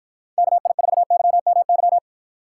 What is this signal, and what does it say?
Morse code, heard as a keyed tone, about 54 wpm, 700 Hz. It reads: KI4YMY